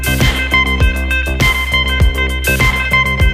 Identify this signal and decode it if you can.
Emergency Alert System (SAME) tones, heard as harsh buzzing bursts. It is heard from 0.3 to 3.1 s.